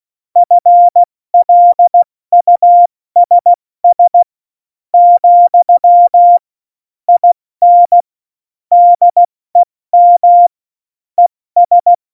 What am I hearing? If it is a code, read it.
Morse code, heard as a keyed tone, about 16 wpm, 700 Hz. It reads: FLUSS , IN DEM ES
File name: Morsekurs / Witze / Witz 30. Oktober 2018, 70 F80